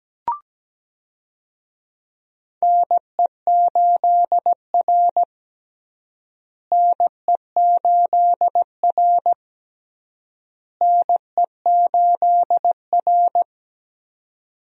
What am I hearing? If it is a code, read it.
Morse code, heard as a keyed tone, about 17 wpm, 700 Hz. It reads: NE8R NE8R NE8R
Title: USA Call Signs - 4 Characters - Repeated 3x - 17wpm